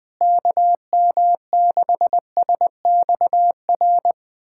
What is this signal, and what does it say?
Morse code, heard as a keyed tone, about 20 wpm, 700 Hz. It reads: KM6SXR
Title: USA Callsign DE Callsign - 6 Characters - 20wpm